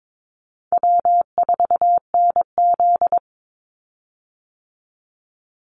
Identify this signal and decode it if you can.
Morse code, heard as a keyed tone, about 22 wpm, 690 Hz. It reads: W4NZ
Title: Call Signs Repeated 3x - 22wpm